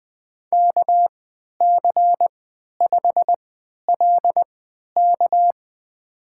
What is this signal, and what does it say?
Morse code, heard as a keyed tone, about 21 wpm, spaced as for 11 wpm, 700 Hz. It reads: KC5LK
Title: USA Call Signs - 5 Characters - 3x Character Spacing - 20wpm